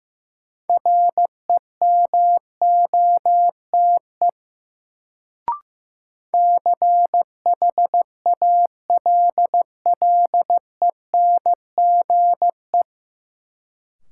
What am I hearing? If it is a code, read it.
Morse code, heard as a keyed tone, about 15 wpm, 700 Hz. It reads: REMOTE CHALLENGE
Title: Sentences from Top 1300 Words - New Word Drill Alt - 15wpm